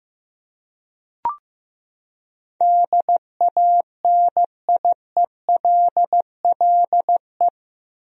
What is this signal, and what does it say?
Morse code, heard as a keyed tone, about 15 wpm, 700 Hz. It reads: DANIELLE
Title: USA Common Names - 15wpm